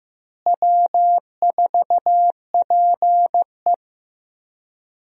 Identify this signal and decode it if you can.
Morse code, heard as a keyed tone, about 15 wpm, 700 Hz. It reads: W4PE
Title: USA Call Signs - 4 Characters - Repeated 3x - 15wpm